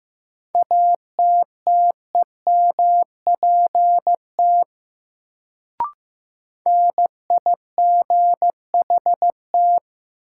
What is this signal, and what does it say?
Morse code, heard as a keyed tone, about 15 wpm, 700 Hz. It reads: ATTEMPT NIGHT
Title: Sets of 4 Words - New Word Drill - 15wpm